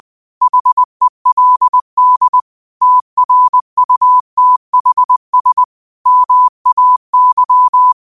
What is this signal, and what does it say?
Morse code, heard as a keyed tone, about 20 wpm, 1000 Hz. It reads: HELD TRUTHS MAY
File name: qotdfiles / 20wpm